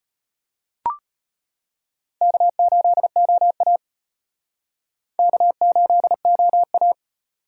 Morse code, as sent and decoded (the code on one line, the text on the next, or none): -.- ---.. --- .- / -.- ---.. --- .-
K8OA K8OA